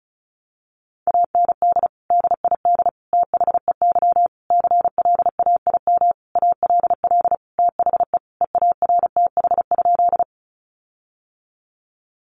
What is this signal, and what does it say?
Morse code, heard as a keyed tone, about 34 wpm, 700 Hz. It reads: AND DID THEY CLAIM ALL THE EARTH?
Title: Sentences from Top 1500 Words 35wpm